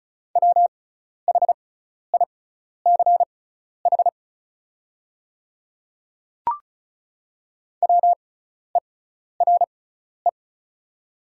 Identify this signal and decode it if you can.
Morse code, heard as a keyed tone, about 36 wpm, spaced as for 12 wpm, 700 Hz. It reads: WHICH WERE
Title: 6x Character Spacing - Top 200 Words - 35wpm